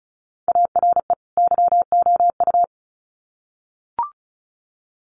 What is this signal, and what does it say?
Morse code, heard as a keyed tone, about 34 wpm, 700 Hz. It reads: ARE YOU
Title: Sets of 2 Words 35wpm